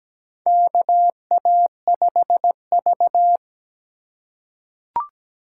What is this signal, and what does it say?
Morse code, heard as a keyed tone, about 17 wpm, 700 Hz. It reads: KA5V
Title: USA Call Signs - 4 Characters - 3x Character Spacing - 17wpm